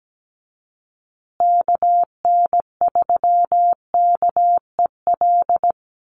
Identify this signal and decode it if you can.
Morse code, heard as a keyed tone, about 17 wpm, 700 Hz. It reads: KN3KEL
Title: USA Call Signs Repeated 3x 17wpm